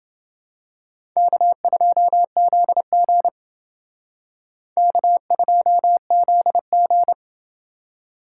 Morse code, heard as a keyed tone, about 28 wpm, 700 Hz. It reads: K2ZG K2ZG